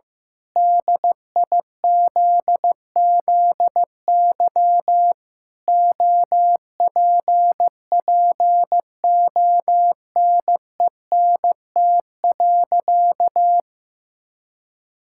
Morse code, heard as a keyed tone, about 15 wpm, 700 Hz. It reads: DIZZY OPPONENT.
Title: Pangram Challenge v2 - 15wpm